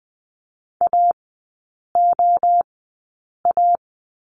Morse code, heard as a keyed tone, about 20 wpm, spaced as for 8 wpm, 700 Hz. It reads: AOA
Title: 041 Lesson 8e - Number 1 - Three Characters 20wpm